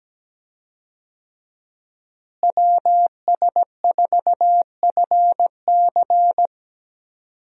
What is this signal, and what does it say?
Morse code, heard as a keyed tone, about 17 wpm, 700 Hz. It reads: WS4FC
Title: USA Call Signs - 5 Characters - Repeated 3x - 17wpm